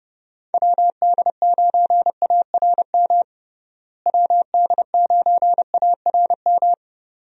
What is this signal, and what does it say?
Morse code, heard as a keyed tone, about 32 wpm, 700 Hz. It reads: WD9ARM WD9ARM
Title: USA Call Signs - 6 Characters - Repeated 2x - 30wpm